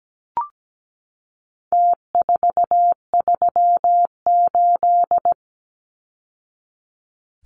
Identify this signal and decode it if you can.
Morse code, heard as a keyed tone, about 17 wpm, 700 Hz. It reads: T438
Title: Numbers - Contest Style - 4-Digit 17wpm